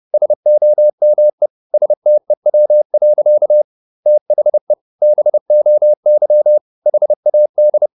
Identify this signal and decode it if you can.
Morse code, heard as a keyed tone, about 30 wpm, 600 Hz. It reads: SOME STEW. THE BOY HAD
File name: the old man and the sea / the old man and the sea 1 30wpm